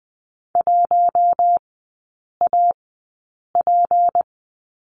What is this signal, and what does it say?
Morse code, heard as a keyed tone, about 20 wpm, spaced as for 8 wpm, 700 Hz. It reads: 1AP
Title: 152 Lesson 25e - Letter p - Three Characters 20wpm